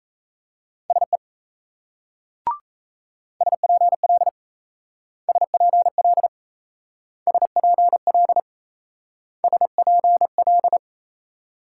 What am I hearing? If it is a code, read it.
Morse code, heard as a keyed tone, about 38 wpm, 700 Hz. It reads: SE SPL SPL SPL SPL